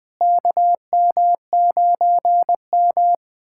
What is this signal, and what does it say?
Morse code, heard as a keyed tone, about 20 wpm, 700 Hz. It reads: KM9M
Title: K1USN SST QSO Practice - 20wpm